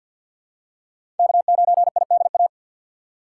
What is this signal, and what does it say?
Morse code, heard as a keyed tone, about 48 wpm, 700 Hz. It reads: K9IDA
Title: USA Call Signs - 5 Characters - 6x Character Spacing - 50wpm